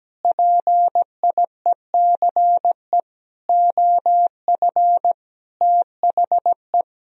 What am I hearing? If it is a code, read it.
Morse code, heard as a keyed tone, about 17 wpm, 700 Hz. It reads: PIECE OF THE
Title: Sentences from Top 1000 Words - Review Part 2 - 17wpm